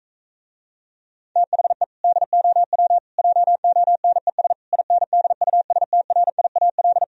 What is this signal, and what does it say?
Morse code, heard as a keyed tone, about 45 wpm, 700 Hz. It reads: THE DOW JONES INDUSTRIAL